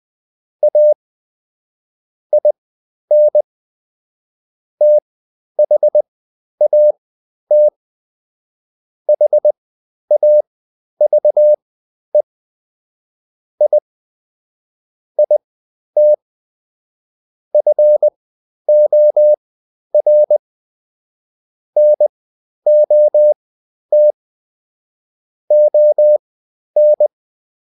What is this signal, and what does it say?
Morse code, heard as a keyed tone, about 20 wpm, spaced as for 11 wpm, 600 Hz. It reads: A IN THAT HAVE I IT FOR NOT ON
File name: Top-100-Words-10